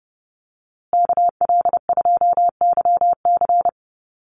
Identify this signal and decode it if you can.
Morse code, heard as a keyed tone, about 30 wpm, 700 Hz. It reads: KL2YC
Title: USA Call Signs Repeated 3x 30wpm